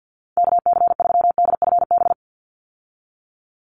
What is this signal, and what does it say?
Morse code, heard as a keyed tone, about 49 wpm, 700 Hz. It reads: KC3DFB
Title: USA Call Signs 50wpm